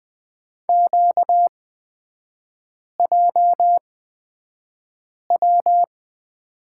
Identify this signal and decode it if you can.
Morse code, heard as a keyed tone, about 20 wpm, spaced as for 5 wpm, 700 Hz. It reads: QJW
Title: 243 Lesson 39e - Letter z - Three Characters 20wpm